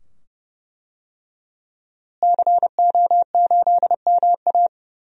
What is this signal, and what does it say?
Morse code, heard as a keyed tone, about 32 wpm, 700 Hz. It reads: CO8MA
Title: DXCC-100 Call Signs - 5 Characters - Repeated 2x - 30wpm